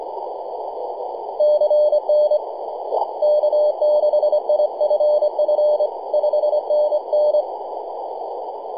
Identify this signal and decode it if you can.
Morse code, heard as a keyed tone, about 25 wpm, 600 Hz, with noise. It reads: CN K6IFF 5NN
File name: e6zs-21-009mhz-cw